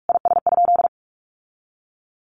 Morse code, heard as a keyed tone, about 44 wpm, 700 Hz. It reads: IS?
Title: Sentences from Top 1500 Words 45wpm